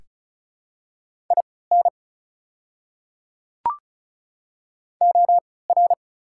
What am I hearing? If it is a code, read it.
Morse code, heard as a keyed tone, about 35 wpm, spaced as for 20 wpm, 700 Hz. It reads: IN OR
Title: 3x Character Spacing - Top 100 Words - 35wpm